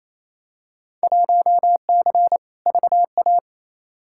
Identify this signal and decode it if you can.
Morse code, heard as a keyed tone, about 28 wpm, 700 Hz. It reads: 1C VA
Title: Field Day Exchanges Encore - 28wpm